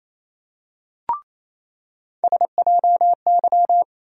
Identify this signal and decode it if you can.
Morse code, heard as a keyed tone, about 28 wpm, 700 Hz. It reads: SJY
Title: USA Call Sign Postfixes - 3 Letters 28wpm